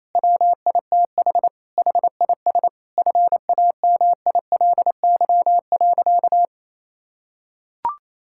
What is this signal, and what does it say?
Morse code, heard as a keyed tone, about 28 wpm, 700 Hz. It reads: WITH HIS FAMILY.
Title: Sentences from Top 1600 Words 28wpm